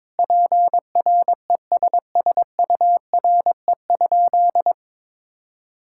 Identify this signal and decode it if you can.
Morse code, heard as a keyed tone, about 22 wpm, 700 Hz. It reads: PRESSURE?
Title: Sentences from Top 1000 Words - Review Part 6 - 22wpm